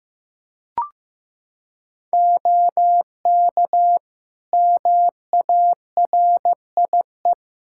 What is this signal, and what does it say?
Morse code, heard as a keyed tone, about 15 wpm, 700 Hz. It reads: OK MARIE
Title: Practice QSO - Exchange 3 - More Info - 15wpm